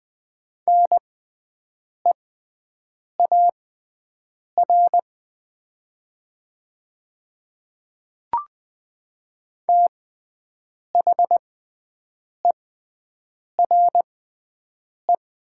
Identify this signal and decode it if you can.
Morse code, heard as a keyed tone, about 20 wpm, spaced as for 7 wpm, 700 Hz. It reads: NEAR THERE
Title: 6x Character Spacing - Top 200 Words - 20wpm